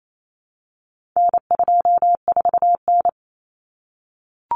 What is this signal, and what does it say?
Morse code, heard as a keyed tone, about 28 wpm, 700 Hz. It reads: N24N
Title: Common Cut Numbers - 4-Digit 28wpm